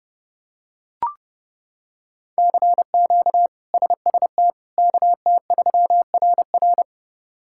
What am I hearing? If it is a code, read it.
Morse code, heard as a keyed tone, about 30 wpm, 700 Hz. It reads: CQ SST KT3RR
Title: K1USN SST QSO Practice - 30wpm